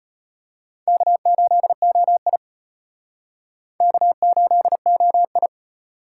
Morse code, heard as a keyed tone, about 32 wpm, 700 Hz. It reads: K8OI K8OI